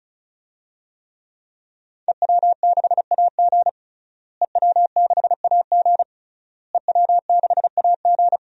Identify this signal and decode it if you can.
Morse code, heard as a keyed tone, about 36 wpm, 700 Hz. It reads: EW6AG EW6AG EW6AG
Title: DXCC-100 Call Signs - 5 Characters - Repeated 3x - 35wpm